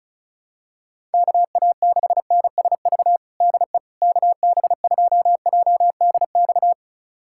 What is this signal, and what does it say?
Morse code, heard as a keyed tone, about 36 wpm, 700 Hz. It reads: KA6NSV DE KB2JDX
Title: USA Callsign DE Callsign 35wpm